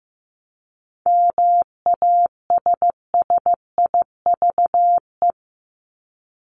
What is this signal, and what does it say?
Morse code, heard as a keyed tone, about 15 wpm, 700 Hz. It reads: MASSIVE
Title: Top 1751-2000 Words 15wpm